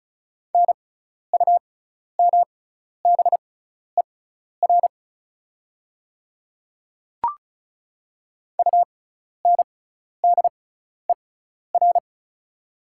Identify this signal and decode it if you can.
Morse code, heard as a keyed tone, about 36 wpm, spaced as for 12 wpm, 700 Hz. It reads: NUMBER UNDER
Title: 6x Character Spacing - Top 300 Words - 35wpm